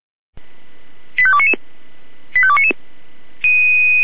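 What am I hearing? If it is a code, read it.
Five-tone selective call (ZVEI), heard as quick runs of five tones, tones of about 70 ms. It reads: 85299 85299
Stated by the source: ZVEI five-tone selective call received on SDR.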